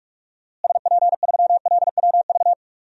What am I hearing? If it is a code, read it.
Morse code, heard as a keyed tone, about 45 wpm, 700 Hz. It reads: SP3LWV
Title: DXCC-100 Call Signs - 6 Characters - Repeated 2x - 45wpm